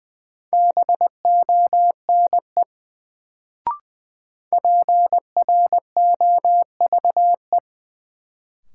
Morse code, heard as a keyed tone, about 20 wpm, 700 Hz. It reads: BONE PROVE
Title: Sentences from Top 1600 Words - New Word Drill - 20wpm